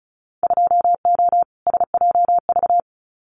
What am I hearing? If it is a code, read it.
Morse code, heard as a keyed tone, about 36 wpm, 700 Hz. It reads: 2O SJV
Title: Winter Field Day Exchanges Encore 35wpm